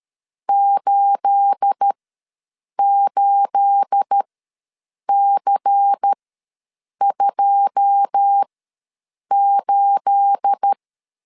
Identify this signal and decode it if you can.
Morse code, heard as a keyed tone, about 13 wpm, spaced as for 7 wpm, 800 Hz. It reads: 88C28
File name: LEZIONI / 9 28 ws 12wpm 3354